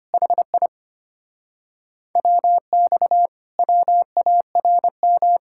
Morse code, heard as a keyed tone, about 30 wpm, 700 Hz. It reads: HI WX WARM
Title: Speed-Racing Common QSO Phrases - 25wpm